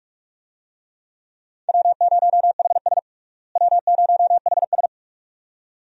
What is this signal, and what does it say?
Morse code, heard as a keyed tone, about 42 wpm, 700 Hz. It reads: W0HS W0HS
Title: USA Call Signs - 4 Characters - Repeated 2x - 45wpm